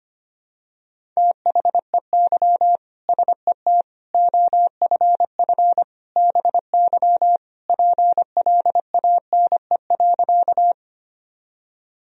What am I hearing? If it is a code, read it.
Morse code, heard as a keyed tone, about 25 wpm, 700 Hz. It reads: THEY SET OFF BY PLANE.
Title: Sentences from Top 1000 Words - Review Part 3 - 25wpm